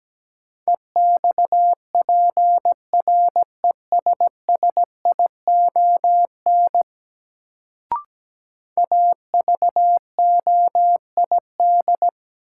Morse code, heard as a keyed tone, about 17 wpm, 700 Hz. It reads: EXPRESSION AVOID
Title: Sentences from Top 1200 Words - New Word Drill Alt - 17wpm